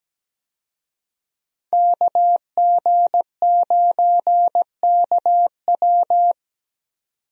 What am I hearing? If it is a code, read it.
Morse code, heard as a keyed tone, about 17 wpm, 700 Hz. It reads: KG9KW
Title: USA Call Signs - 5 Characters - 17wpm